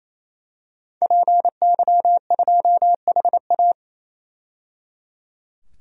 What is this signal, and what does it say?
Morse code, heard as a keyed tone, about 28 wpm, 700 Hz. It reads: PY2HA
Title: DXCC-100 Call Signs - 5 Characters - 28wpm